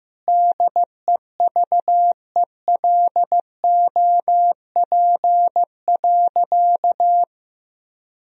Morse code, heard as a keyed tone, about 16 wpm, 700 Hz. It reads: DEVELOP.
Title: Sentences from Top 1000 Words - Review Part 2 - 15wpm